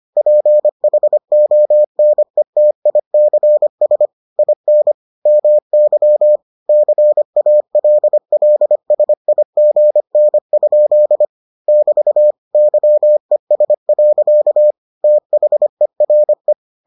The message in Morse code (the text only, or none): PHONETICS IN MY CALLSIGN? <BT> YES. THERE